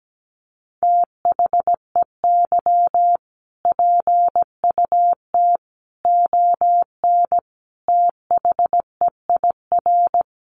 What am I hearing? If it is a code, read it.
Morse code, heard as a keyed tone, about 17 wpm, 700 Hz. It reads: THEY PUT ON THEIR
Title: A Sound Of Thunder 17wpm